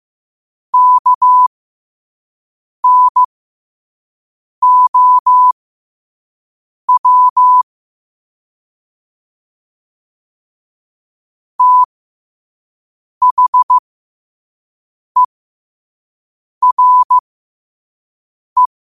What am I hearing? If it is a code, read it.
Morse code, heard as a keyed tone, about 15 wpm, spaced as for 5 wpm, 1000 Hz. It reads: KNOW THERE
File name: qotdfiles / 5wpm